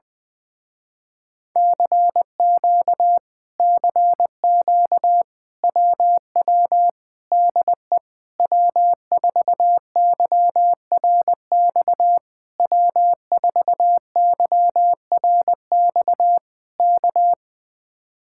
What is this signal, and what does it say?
Morse code, heard as a keyed tone, about 19 wpm, 700 Hz. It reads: CQ CQ WW DE W4YRX W4YRX K
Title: Speed-Racing Common QSO Phrases - 20wpm